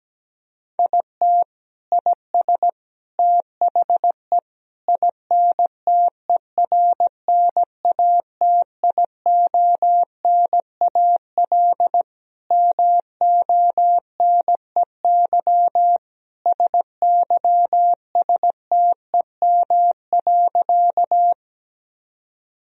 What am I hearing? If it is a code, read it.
Morse code, heard as a keyed tone, about 17 wpm, 700 Hz. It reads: IT IS THE INTERNATIONAL MONEY SYSTEM.
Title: Sentences from Top 1000 Words - Review Part 4 - 17wpm